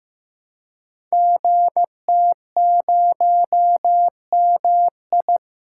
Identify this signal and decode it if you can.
Morse code, heard as a keyed tone, about 16 wpm, 700 Hz. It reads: GT0MI
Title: Code Groups - 5 Letter:Numbers - 15wpm